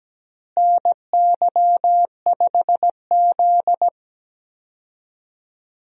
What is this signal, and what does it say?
Morse code, heard as a keyed tone, about 17 wpm, 700 Hz. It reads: NY5Z
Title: USA Call Signs - 4 Characters - Repeated 3x - 17wpm